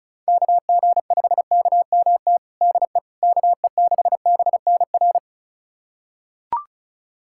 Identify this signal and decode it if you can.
Morse code, heard as a keyed tone, about 34 wpm, 690 Hz. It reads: KG5KMT DE KE6BNR
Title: USA Callsign DE Callsign - 6 Characters - 35wpm